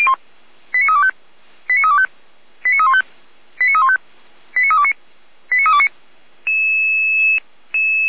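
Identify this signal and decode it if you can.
Five-tone selective call (ZVEI), heard as quick runs of five tones, tones of about 70 ms. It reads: 89325 89325 89315 89315 89329 89329
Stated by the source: ZVEI five-tone selective call 89504 received on an SDR.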